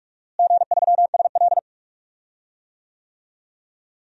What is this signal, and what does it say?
Morse code, heard as a keyed tone, about 46 wpm, 700 Hz. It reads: G3SL
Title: DXCC-100 Call Signs - 4 Characters - Repeated 2x - 45wpm